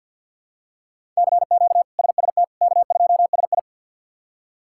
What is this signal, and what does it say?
Morse code, heard as a keyed tone, about 45 wpm, 700 Hz. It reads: CQ SST K2SI
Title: K1USN SST QSO Practice - 50wpm